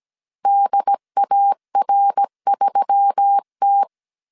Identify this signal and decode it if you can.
Morse code, heard as a keyed tone, about 17 wpm, 800 Hz. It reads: DAR3T
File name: 16wpm